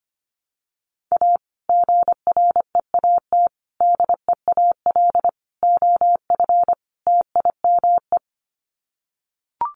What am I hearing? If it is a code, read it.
Morse code, heard as a keyed tone, about 25 wpm, 700 Hz. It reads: A GREAT DEAL OF TIME
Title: Sets of 5 Words Encore2 25wpm